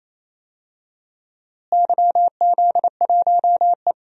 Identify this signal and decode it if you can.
Morse code, heard as a keyed tone, about 28 wpm, 700 Hz. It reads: YZ1E
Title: DXCC-100 Call Signs - 4 Characters - Repeated 2x - 28wpm